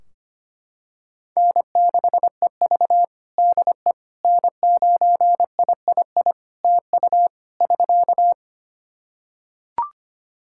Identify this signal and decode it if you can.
Morse code, heard as a keyed tone, about 25 wpm, 700 Hz. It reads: N6EV DE N9III TU <SK>
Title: Practice QSO - Exchange 4 - Ending - 25wpm